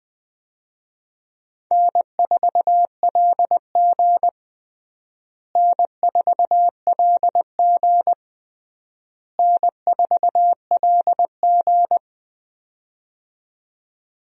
Morse code, heard as a keyed tone, about 20 wpm, 700 Hz. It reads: N4LG N4LG N4LG
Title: USA Call Signs - 4 Characters - Repeated 3x - 20wpm